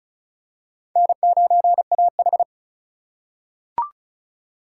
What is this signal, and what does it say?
Morse code, heard as a keyed tone, about 34 wpm, 700 Hz. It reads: N9AH
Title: USA Call Signs - 4 Characters - 4x Character Spacing - 35wpm